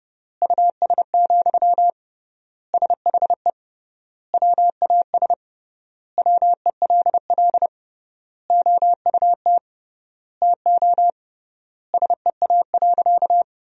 Morse code, heard as a keyed tone, about 30 wpm, 700 Hz. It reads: US, SHE WAS WELL OUT TO SEA.
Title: 3x Word-Spacing - Sentences from Top 400 Words - 30wpm